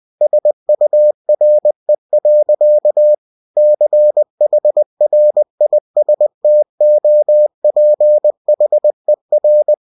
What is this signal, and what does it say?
Morse code, heard as a keyed tone, about 20 wpm, 600 Hz. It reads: SURE. CHRISTOPHER
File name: cw-books / winnie-the-pooh / winnie-20wpm-page-0105